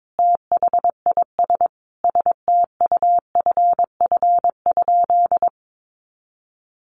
Morse code, heard as a keyed tone, about 22 wpm, 700 Hz. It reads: THIS STUFF?
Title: Sentences from Top 1400 Words 22wpm